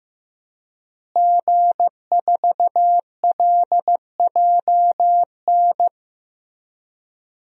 Morse code, heard as a keyed tone, about 16 wpm, 700 Hz. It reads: G4LJN
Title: DXCC-100 Call Signs - 5 Characters - Repeated 2x - 15wpm